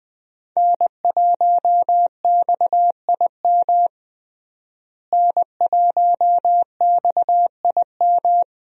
Morse code, heard as a keyed tone, about 20 wpm, 700 Hz. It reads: N1XIM N1XIM
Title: USA Call Signs - 5 Characters - Repeated 2x - 20wpm